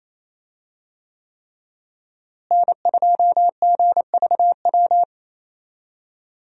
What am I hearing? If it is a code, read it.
Morse code, heard as a keyed tone, about 28 wpm, 700 Hz. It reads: N2GVW
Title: USA Call Signs - 5 Characters - Repeated 3x - 28wpm